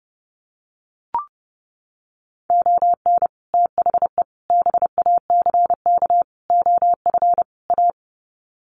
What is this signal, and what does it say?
Morse code, heard as a keyed tone, about 30 wpm, 700 Hz. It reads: ON THE BACK OF A
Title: Sets of 5 Words Encore2 30wpm